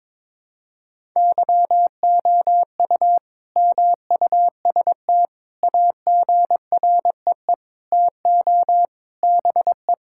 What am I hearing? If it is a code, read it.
Morse code, heard as a keyed tone, about 22 wpm, 700 Hz. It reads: YOU MUST AGREE TO BE
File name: Sentences with 9-Word Phrases - 22wpm